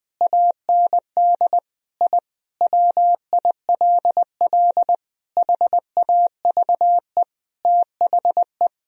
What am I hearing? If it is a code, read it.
Morse code, heard as a keyed tone, about 20 wpm, 700 Hz. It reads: AND I WILL HAVE THE
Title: Sentences from Top 1000 Words - Review Part 3 - 20wpm